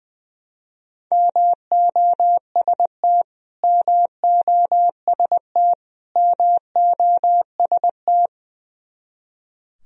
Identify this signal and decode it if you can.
Morse code, heard as a keyed tone, about 20 wpm, 700 Hz. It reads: MOST MOST MOST
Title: Top 100 Words Repeated 20wpm